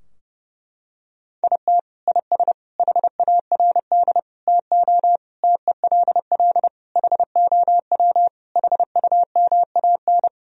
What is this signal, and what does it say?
Morse code, heard as a keyed tone, about 30 wpm, 700 Hz. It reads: IT IS HARD TO TELL HOW HUMAN
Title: Sentences from Top 1000 Words - Review Part 3 - 30wpm